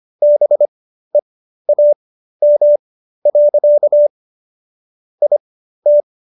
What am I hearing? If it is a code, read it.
Morse code, heard as a keyed tone, about 25 wpm, spaced as for 13 wpm, 590 Hz. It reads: BEAM. IT